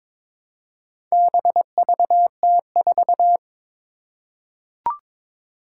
Morse code, heard as a keyed tone, about 22 wpm, 700 Hz. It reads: BVT4
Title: Four Challenging Letter-Number Combinations 22wpm